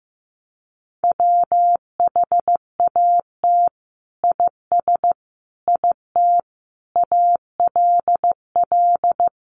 Morse code, heard as a keyed tone, about 15 wpm, 700 Hz. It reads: WHAT IS IT ALL
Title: Sentences from Top 100 Words 15wpm